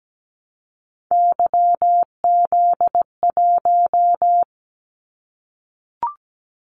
Morse code, heard as a keyed tone, about 17 wpm, 700 Hz. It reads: YZ1
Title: Callsign Prefixes for Top 25 DXCC - 17wpm